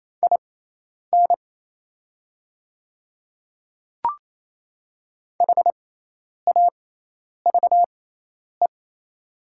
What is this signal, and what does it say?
Morse code, heard as a keyed tone, about 28 wpm, spaced as for 10 wpm, 690 Hz. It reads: IN HAVE
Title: 6x Character Spacing - Top 100 Words - 28wpm